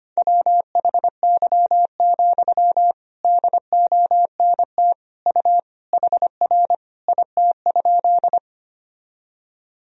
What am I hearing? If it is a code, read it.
Morse code, heard as a keyed tone, about 25 wpm, 700 Hz. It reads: WHY, DONT U HR IT?
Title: Sentences from Top 300 Words and Abbreviations 25wpm